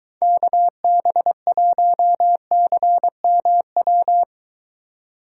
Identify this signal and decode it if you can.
Morse code, heard as a keyed tone, about 22 wpm, 700 Hz. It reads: KB1CMW